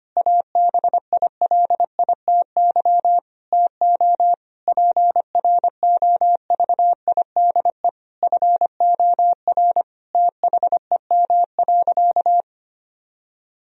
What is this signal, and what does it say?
Morse code, heard as a keyed tone, about 25 wpm, 700 Hz. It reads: ABILITY TO PROVIDE FOR THEM.
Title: Sentences from Top 1000 Words - Review Part 3 - 25wpm